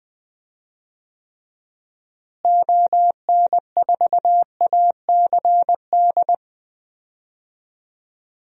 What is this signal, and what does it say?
Morse code, heard as a keyed tone, about 20 wpm, 700 Hz. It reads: ON4ACD
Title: DXCC-100 Call Signs - 6 Characters - Repeated 2x - 20wpm